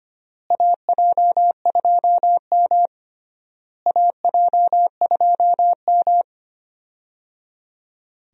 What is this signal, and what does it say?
Morse code, heard as a keyed tone, about 25 wpm, 700 Hz. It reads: AJ2M AJ2M
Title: USA Call Signs - 4 Characters - Repeated 3x - 25wpm